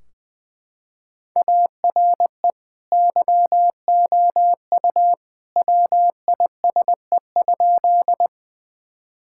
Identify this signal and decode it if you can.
Morse code, heard as a keyed tone, about 21 wpm, 700 Hz. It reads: ARE YOU WISE?
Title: Sentences from Top 1600 Words 20wpm